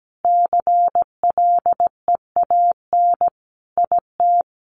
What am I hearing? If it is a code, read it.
Morse code, heard as a keyed tone, about 17 wpm, 700 Hz. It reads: CLEAN IT
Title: Additional Easy Phrases 17wpm